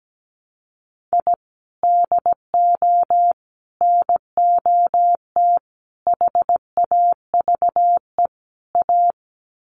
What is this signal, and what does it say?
Morse code, heard as a keyed tone, about 17 wpm, 700 Hz. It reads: I DO NOT HAVE A
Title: Sets of 5 Words 17wpm